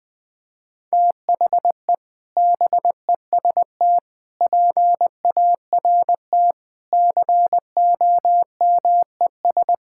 Morse code, heard as a keyed tone, about 20 wpm, 700 Hz. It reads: THE BEST PART COMES
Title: Sentences with 6-Word Phrases - 20wpm